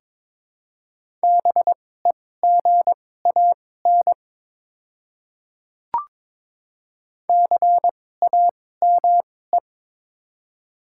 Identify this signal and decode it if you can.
Morse code, heard as a keyed tone, about 22 wpm, spaced as for 16 wpm, 700 Hz. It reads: BEGAN CAME
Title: 2x Character Spacing - Top 300 Words - 22wpm